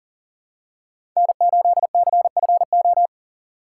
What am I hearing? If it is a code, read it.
Morse code, heard as a keyed tone, about 40 wpm, 700 Hz. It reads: N8CFO
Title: USA Call Signs - 5 Characters - 2x Character Spacing - 40wpm